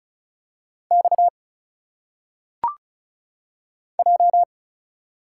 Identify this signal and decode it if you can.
Morse code, heard as a keyed tone, about 34 wpm, 700 Hz. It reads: X J
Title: Single Letters - Phonetic Pronunciation 35wpm